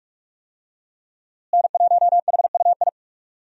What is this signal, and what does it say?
Morse code, heard as a keyed tone, about 45 wpm, 700 Hz. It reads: N1HUI
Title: USA Call Signs - 5 Characters - Repeated 3x - 45wpm